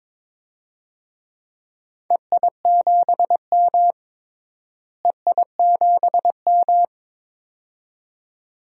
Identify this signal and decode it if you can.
Morse code, heard as a keyed tone, about 21 wpm, 700 Hz. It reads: EI7M EI7M
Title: DXCC-100 Call Signs - 4 Characters - Repeated 2x - 22wpm